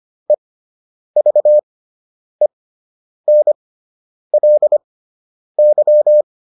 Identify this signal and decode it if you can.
Morse code, heard as a keyed tone, about 25 wpm, spaced as for 9 wpm, 600 Hz. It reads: EVENLY